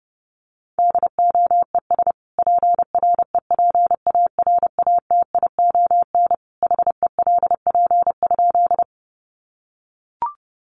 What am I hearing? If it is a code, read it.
Morse code, heard as a keyed tone, about 30 wpm, 700 Hz. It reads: DOES PREPARATION HELP?
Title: Sentences from Top 1400 Words 30wpm